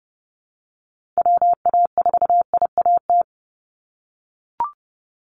Morse code, heard as a keyed tone, about 30 wpm, 700 Hz. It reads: WA4IAT
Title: USA Call Signs 30wpm